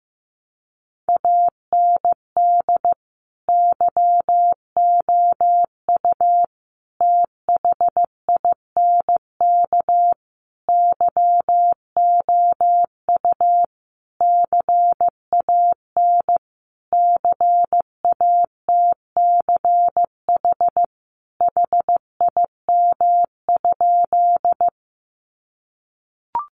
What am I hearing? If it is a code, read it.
Morse code, heard as a keyed tone, about 15 wpm, 700 Hz. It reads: AND YOU THINK YOU CAN CATCH HIM?
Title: Sentences from Top 1500 Words 15wpm